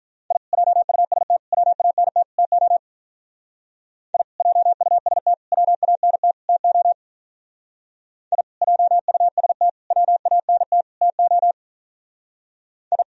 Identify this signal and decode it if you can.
Morse code, heard as a keyed tone, about 45 wpm, 700 Hz. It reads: I JUST WANT TO I JUST WANT TO I JUST WANT TO I